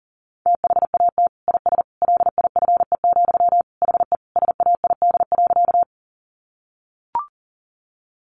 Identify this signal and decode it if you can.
Morse code, heard as a keyed tone, about 40 wpm, 700 Hz. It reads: THAT IS LIFE, HE SAID.
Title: Sentences from Top 900 Words 40wpm